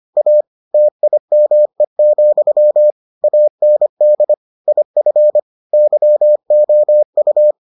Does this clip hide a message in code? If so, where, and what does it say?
Yes: Morse code from 0.2 to 7.5 s, heard as a keyed tone, about 25 wpm, 600 Hz. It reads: A TIME, AND IF YOU